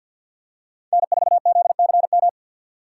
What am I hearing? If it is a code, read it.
Morse code, heard as a keyed tone, about 51 wpm, 700 Hz. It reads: N4ZCM
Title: USA Call Signs - 5 Characters - 2x Character Spacing - 50wpm